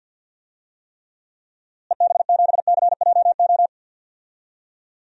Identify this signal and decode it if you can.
Morse code, heard as a keyed tone, about 52 wpm, 700 Hz. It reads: EB7CJO
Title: DXCC-100 Call Signs - 6 Characters - Repeated 3x - 50wpm